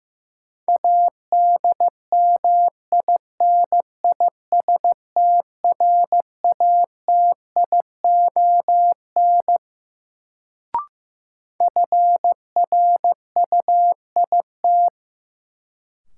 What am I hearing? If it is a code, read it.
Morse code, heard as a keyed tone, about 15 wpm, 700 Hz. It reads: ADMINISTRATION FRUIT
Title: Sentences from Top 1300 Words - New Word Drill Alt - 15wpm